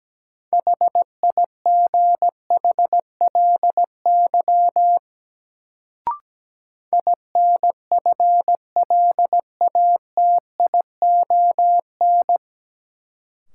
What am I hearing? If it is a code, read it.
Morse code, heard as a keyed tone, about 17 wpm, 700 Hz. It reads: HIGHLY INFLATION
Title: Sentences from Top 1100 Words - New Word Drill Alt - 17wpm